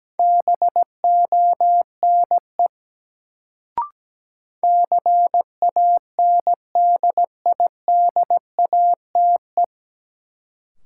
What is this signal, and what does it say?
Morse code, heard as a keyed tone, about 17 wpm, 700 Hz. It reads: BONE CANDIDATE
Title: Sentences from Top 1600 Words - New Word Drill - 17wpm